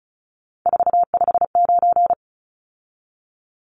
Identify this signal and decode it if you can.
Morse code, heard as a keyed tone, about 34 wpm, 700 Hz. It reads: 459
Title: Numbers - 3-Digit 35wpm